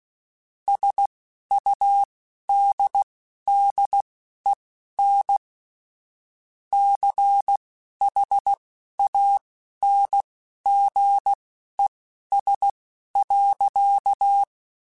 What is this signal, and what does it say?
Morse code, heard as a keyed tone, about 16 wpm, spaced as for 11 wpm, 800 Hz. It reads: SUDDEN CHANGES.